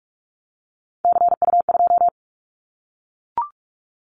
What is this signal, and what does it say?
Morse code, heard as a keyed tone, about 44 wpm, 700 Hz. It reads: CU2
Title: Callsign Prefixes for Top 100 DXCC - 45wpm